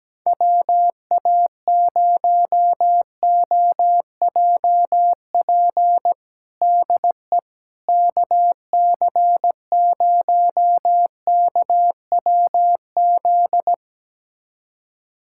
Morse code, heard as a keyed tone, about 17 wpm, 700 Hz. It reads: WA0OJP DE KC0KWZ
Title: USA Callsign DE Callsign - 6 Characters - 17wpm